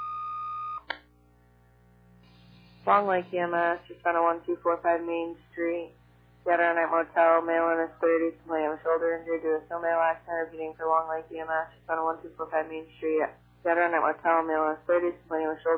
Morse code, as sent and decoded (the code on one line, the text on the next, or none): none